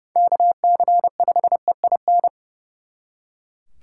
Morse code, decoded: KC5EIN

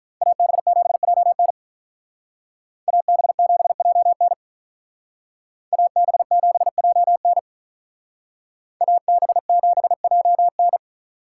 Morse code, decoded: AB7JN AB7JN AB7JN AB7JN

